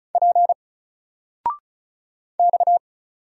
Morse code, heard as a keyed tone, about 35 wpm, 700 Hz. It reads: P X
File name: Single-Letter-Number-35wpm